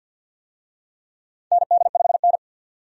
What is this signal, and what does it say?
Morse code, heard as a keyed tone, about 49 wpm, 700 Hz. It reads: ND5N
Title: USA Call Signs - 4 Characters - 50wpm